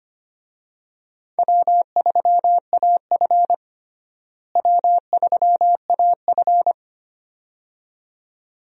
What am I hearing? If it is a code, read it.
Morse code, heard as a keyed tone, about 25 wpm, 700 Hz. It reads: W3AF W3AF
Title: USA Call Signs - 4 Characters - Repeated 2x - 25wpm